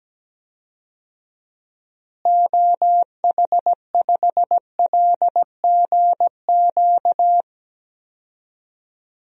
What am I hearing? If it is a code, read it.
Morse code, heard as a keyed tone, about 17 wpm, 700 Hz. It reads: OH5LGQ
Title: DXCC-100 Call Signs - 6 Characters - Repeated 2x - 17wpm